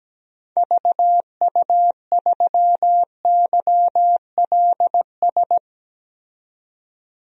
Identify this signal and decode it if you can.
Morse code, heard as a keyed tone, about 17 wpm, 700 Hz. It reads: VU3YLS
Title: DXCC-100 Call Signs - 6 Characters - 17wpm